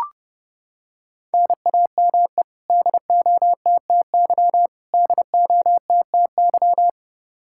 Morse code, heard as a keyed tone, about 30 wpm, 700 Hz. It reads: NAME DOTTY DOTTY
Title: Practice QSO - Exchange 1 - Beginning - 30wpm